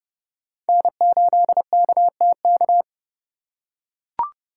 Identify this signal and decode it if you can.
Morse code, heard as a keyed tone, about 28 wpm, 700 Hz. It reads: N8KTK